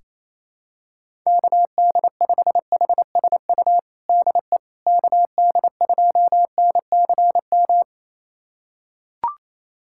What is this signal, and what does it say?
Morse code, heard as a keyed tone, about 28 wpm, 700 Hz. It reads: KD5HSU DE KD2NCM
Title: USA Callsign DE Callsign - 6 Characters - 28wpm